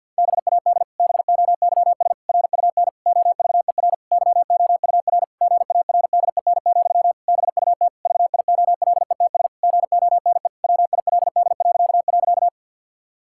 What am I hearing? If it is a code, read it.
Morse code, heard as a keyed tone, about 50 wpm, 700 Hz. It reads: BAD BOYS RUN OVER YOUR GARDEN, BUT VIOLETS GONE WILD..
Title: Pangram Challenge v2 - 50wpm